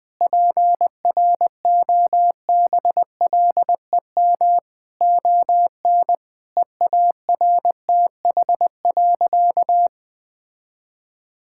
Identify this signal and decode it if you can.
Morse code, heard as a keyed tone, about 20 wpm, 700 Hz. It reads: PROBLEM ON EARTH.
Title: Sentences from Top 1000 Words - Review Part 6 - 20wpm